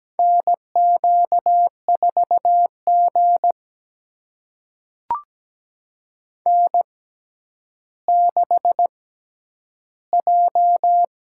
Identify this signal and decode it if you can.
Morse code, heard as a keyed tone, about 17 wpm, spaced as for 10 wpm, 700 Hz. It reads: NQ4G N6J
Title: USA Call Signs - 4 Characters - 6x Character Spacing - 17wpm